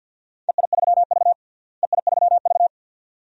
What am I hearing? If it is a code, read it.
Morse code, heard as a keyed tone, about 51 wpm, 700 Hz. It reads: EI3V EI3V
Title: DXCC-100 Call Signs - 4 Characters - Repeated 2x - 50wpm